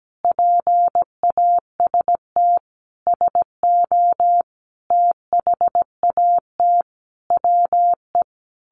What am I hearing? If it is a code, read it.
Morse code, heard as a keyed tone, about 17 wpm, 700 Hz. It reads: PAST SO THAT WE
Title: A Sound Of Thunder 17wpm